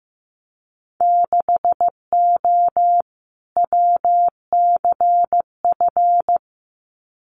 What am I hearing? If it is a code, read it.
Morse code, heard as a keyed tone, about 15 wpm, 700 Hz. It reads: 6O WCF
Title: Winter Field Day Exchanges Encore 15wpm